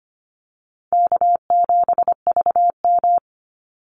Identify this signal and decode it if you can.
Morse code, heard as a keyed tone, about 25 wpm, 700 Hz. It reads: K7VM
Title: Call Signs - 25wpm